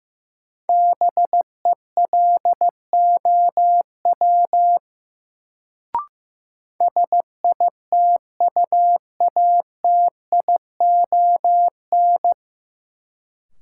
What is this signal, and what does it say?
Morse code, heard as a keyed tone, about 15 wpm, 700 Hz. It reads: BELOW SITUATION
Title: Sentences from Top 1200 Words - New Word Drill Alt - 15wpm